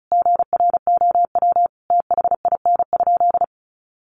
Morse code, heard as a keyed tone, about 34 wpm, 700 Hz. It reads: GROW THIN?
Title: Sentences from Top 1200 Words 35wpm